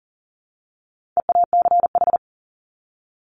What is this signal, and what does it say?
Morse code, heard as a keyed tone, about 40 wpm, 700 Hz. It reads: EACH
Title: Top 200 Words 40wpm